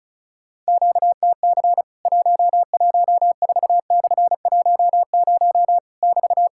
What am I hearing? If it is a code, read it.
Morse code, heard as a keyed tone, about 36 wpm, 700 Hz. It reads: QTC 114/10 <BT>
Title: Worked All Europe DX Contest - QTC Exchange - 35wpm